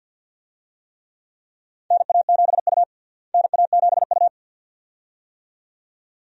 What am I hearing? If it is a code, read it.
Morse code, heard as a keyed tone, about 52 wpm, 700 Hz. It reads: NA7U NA7U
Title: USA Call Signs - 4 Characters - Repeated 2x - 50wpm